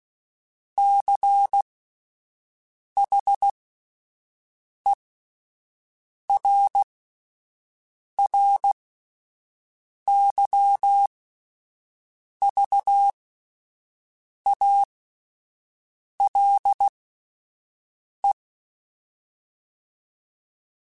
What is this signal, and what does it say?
Morse code, heard as a keyed tone, about 16 wpm, spaced as for 5 wpm, 800 Hz. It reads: CHERRYVALE
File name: fromSMs / FromSMs mp3s / Cities@5